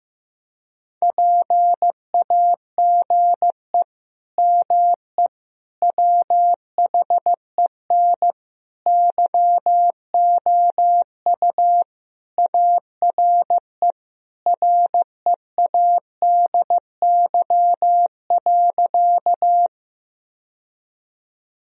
Morse code, heard as a keyed tone, about 15 wpm, 700 Hz. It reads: PAGE ME WHEN YOU ARE READY.
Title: Sentences from Top 1000 Words - Review Part 1 - 15wpm